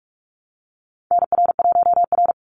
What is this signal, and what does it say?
Morse code, heard as a keyed tone, about 44 wpm, 700 Hz. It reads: NR1R
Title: Call Signs Repeated 3x - 45wpm